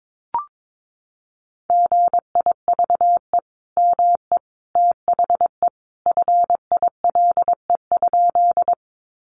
Morse code, heard as a keyed tone, about 22 wpm, 700 Hz. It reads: GIVE ME THE FILE?
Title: Sentences from Top 1200 Words 22wpm